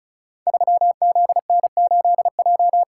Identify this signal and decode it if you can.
Morse code, heard as a keyed tone, about 36 wpm, 700 Hz. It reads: 3ZN8J
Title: Code Groups - 5 Letter:Numbers - 35wpm